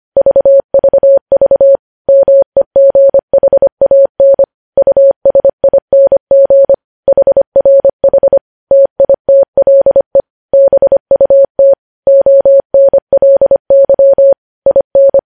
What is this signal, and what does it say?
Morse code, heard as a keyed tone, about 25 wpm, 550 Hz. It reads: VVV MEGHAN USING HRH TITLE BUT ONLY IN
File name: podcast / News headlines-1x-25x25